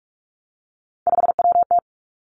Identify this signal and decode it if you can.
Morse code, heard as a keyed tone, about 46 wpm, 700 Hz. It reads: 5WT
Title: Three Challenging Letter-Number Combinations 45wpm